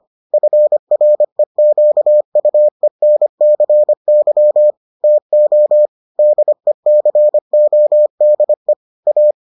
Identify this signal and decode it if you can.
Morse code, heard as a keyed tone, about 25 wpm, 600 Hz. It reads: FREQUENCY TO DECODE A